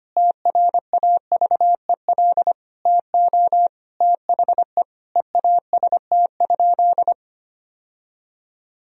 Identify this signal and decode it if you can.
Morse code, heard as a keyed tone, about 25 wpm, 700 Hz. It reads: TRAVEL TO THE EAST?
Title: Sentences from Top 1000 Words - Review Part 3 - 25wpm